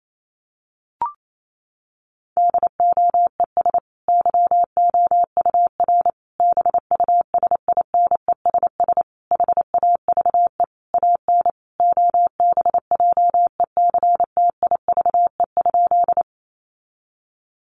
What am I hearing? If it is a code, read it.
Morse code, heard as a keyed tone, about 28 wpm, 700 Hz. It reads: DOES YOUR BUSINESS HAVE AN OBJECTIVE?
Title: Sentences from Top 1200 Words 28wpm